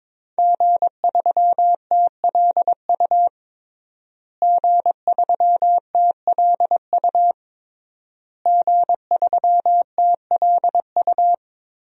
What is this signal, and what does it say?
Morse code, heard as a keyed tone, about 22 wpm, 700 Hz. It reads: G3TLU G3TLU G3TLU
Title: DXCC-100 Call Signs - 5 Characters - Repeated 3x - 22wpm